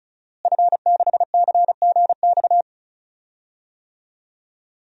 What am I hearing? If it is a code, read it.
Morse code, heard as a keyed tone, about 36 wpm, 690 Hz. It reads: F6CGX
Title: DXCC-100 Call Signs - 5 Characters - Repeated 3x - 35wpm